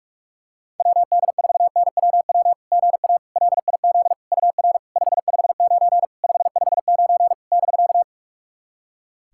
Morse code, heard as a keyed tone, about 44 wpm, 690 Hz. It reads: WD4NWW GA LIZ UR 559 559 <BK>
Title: SOTA QSO Practice 45wpm